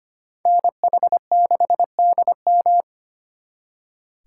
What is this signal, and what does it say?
Morse code, heard as a keyed tone, about 32 wpm, 700 Hz. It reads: NH6DM